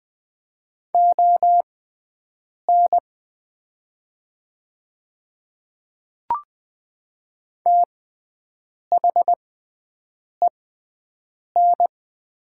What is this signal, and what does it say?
Morse code, heard as a keyed tone, about 20 wpm, spaced as for 7 wpm, 700 Hz. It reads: ON THEN
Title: 6x Character Spacing - Top 100 Words - 20wpm